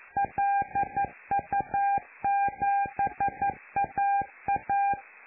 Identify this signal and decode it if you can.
Morse code, heard as a keyed tone, about 11 wpm, 800 Hz. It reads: LU7AA
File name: lu4aao / Globo Amsat